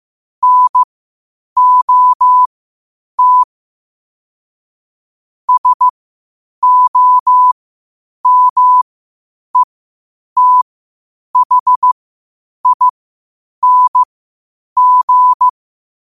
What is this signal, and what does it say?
Morse code, heard as a keyed tone, about 15 wpm, spaced as for 8 wpm, 1000 Hz. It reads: NOT SOMETHING
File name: qotdfiles / 8wpm